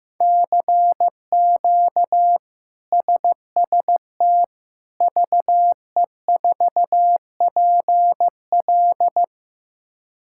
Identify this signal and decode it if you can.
Morse code, heard as a keyed tone, about 15 wpm, 700 Hz. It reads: CQ SST VE4PL
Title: K1USN SST QSO Practice - 15wpm